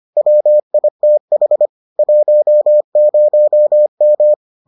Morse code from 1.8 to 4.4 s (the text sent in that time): 10M